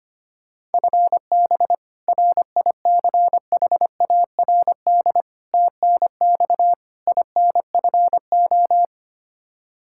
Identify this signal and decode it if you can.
Morse code, heard as a keyed tone, about 25 wpm, 700 Hz. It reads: FB RICHARD TNX INFO
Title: Practice QSO - Exchange 3 - More Info - 25wpm